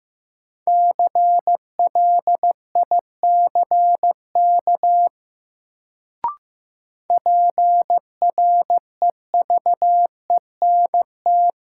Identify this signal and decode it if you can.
Morse code, heard as a keyed tone, about 15 wpm, 700 Hz. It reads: CLICK PREVENT
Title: Sentences from Top 1100 Words - New Word Drill Alt - 15wpm